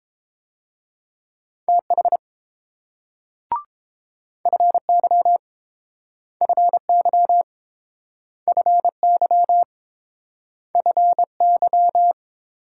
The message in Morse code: TH FY FY FY FY

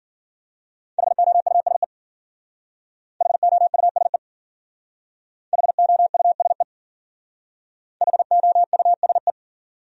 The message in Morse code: HOUSE HOUSE HOUSE HOUSE

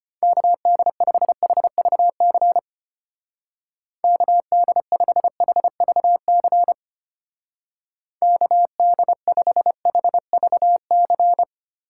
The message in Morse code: KD5HVC KD5HVC KD5HVC

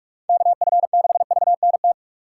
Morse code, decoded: KF6VNT

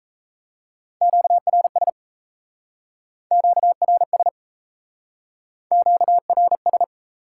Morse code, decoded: QRS QRS QRS

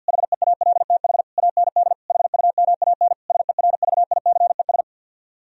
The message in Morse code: HEALTH AND HUMAN SERVICES

